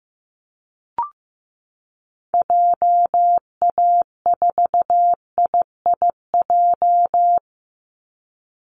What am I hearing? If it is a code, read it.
Morse code, heard as a keyed tone, about 15 wpm, 700 Hz. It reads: JA4IIJ
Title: Call Signs Repeated 2x - 15wpm